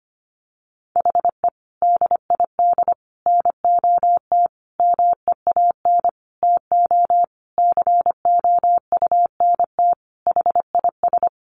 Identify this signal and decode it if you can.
Morse code, heard as a keyed tone, about 25 wpm, 700 Hz. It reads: HE DID NOT MEAN TO COUNT HIS
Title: Sentences from Top 1500 Words 25wpm